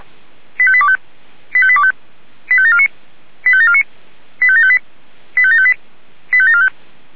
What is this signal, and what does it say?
Five-tone selective call (ZVEI), heard as quick runs of five tones, tones of about 70 ms. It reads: 85825 85825 85749 85749 85758 85758 85745